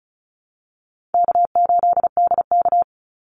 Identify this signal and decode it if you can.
Morse code, heard as a keyed tone, about 34 wpm, 700 Hz. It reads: K8DK
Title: Call Signs Repeated 2x - 35wpm